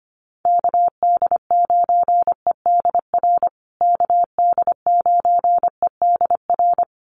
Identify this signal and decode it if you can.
Morse code, heard as a keyed tone, about 25 wpm, 700 Hz. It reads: KD9EDR KD9EDR
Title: USA Call Signs Repeated 2x 25wpm